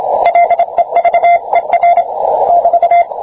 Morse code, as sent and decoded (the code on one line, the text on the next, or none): .-.. ...- . .-. / -...-
LEVER <BT>